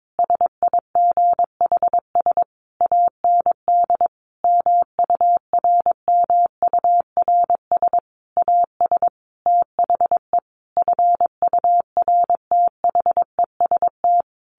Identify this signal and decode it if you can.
Morse code, heard as a keyed tone, about 22 wpm, 700 Hz. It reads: SIGHS AND MURMURS AS THE FURTHEST
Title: A Sound Of Thunder 22wpm